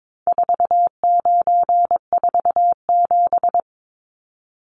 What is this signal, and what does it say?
Morse code, heard as a keyed tone, about 22 wpm, 700 Hz. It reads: 4947
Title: Numbers - 4-Digit 22wpm